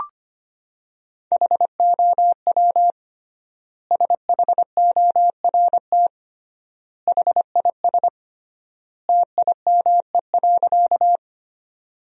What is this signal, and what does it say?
Morse code, heard as a keyed tone, about 25 wpm, 700 Hz. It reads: HOW SHORT HIS TIME.
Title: 3x Word-Spacing - Sentences from Top 400 Words - 25wpm